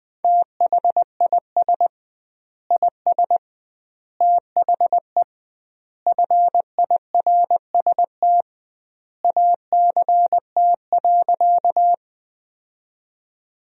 Morse code, heard as a keyed tone, about 20 wpm, 700 Hz. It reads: THIS IS THE FIRST ACT.
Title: 2x Word-Spacing - Sentences from Top 200 Words - 20wpm